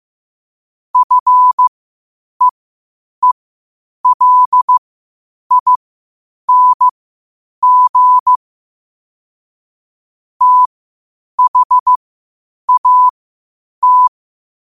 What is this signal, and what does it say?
Morse code, heard as a keyed tone, about 15 wpm, spaced as for 8 wpm, 1000 Hz. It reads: FEELING THAT
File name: qotdfiles / 8wpm